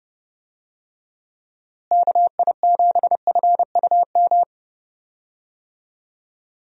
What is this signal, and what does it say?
Morse code, heard as a keyed tone, about 30 wpm, 700 Hz. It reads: KI7FUM
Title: USA Call Signs - 6 Characters - Repeated 2x - 30wpm